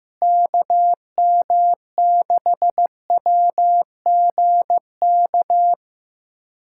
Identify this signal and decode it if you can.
Morse code, heard as a keyed tone, about 15 wpm, 700 Hz. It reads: KM6WGK
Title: USA Callsign DE Callsign 15wpm